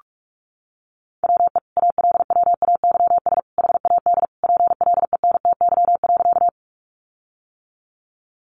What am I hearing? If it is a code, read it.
Morse code, heard as a keyed tone, about 44 wpm, 700 Hz. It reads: WE ALWAYS HAD PLENTY.
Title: Sentences from Top 1300 Words 45wpm